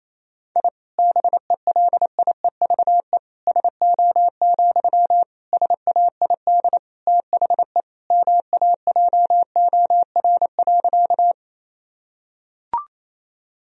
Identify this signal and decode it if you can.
Morse code, heard as a keyed tone, about 27 wpm, 700 Hz. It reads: I BELIEVE SO, SAID THE MAJOR.
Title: Sentences from Top 1000 Words - Review Part 4 - 28wpm